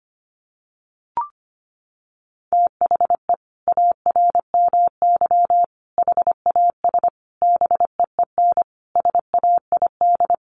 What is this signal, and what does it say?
Morse code, heard as a keyed tone, about 25 wpm, 700 Hz. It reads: THE ARMY HAS BEEN SAID
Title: Sentences from Top 1400 Words 25wpm